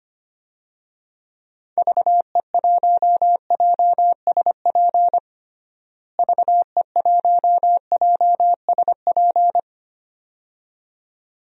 Morse code, heard as a keyed tone, about 25 wpm, 700 Hz. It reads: VE1JSP VE1JSP
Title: DXCC-100 Call Signs - 6 Characters - Repeated 2x - 25wpm